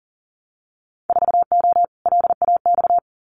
Morse code, heard as a keyed tone, about 40 wpm, 700 Hz. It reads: 4O LAX
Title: Winter Field Day Exchanges Encore 40wpm